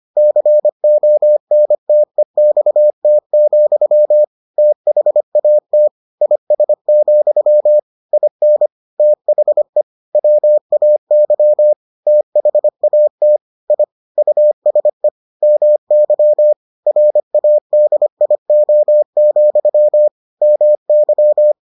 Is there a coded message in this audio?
Yes: Morse code from 0.2 to 21.5 s, heard as a keyed tone, about 25 wpm, 600 Hz. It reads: CONTEXT, THAT IS, IN THE WAY THAT I USE MY RADIO, MY